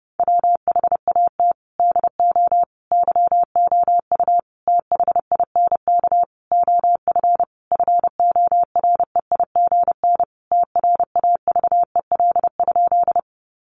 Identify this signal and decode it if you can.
Morse code, heard as a keyed tone, about 30 wpm, 700 Hz. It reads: WHAT DO YOU THINK OF FOREIGN TRAVEL?
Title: Sentences from Top 1400 Words 30wpm